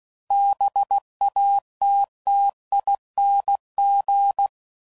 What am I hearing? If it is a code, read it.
Morse code, heard as a keyed tone, about 16 wpm, 800 Hz. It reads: BATTING
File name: Suffixes / Special INGs / Sngl Ings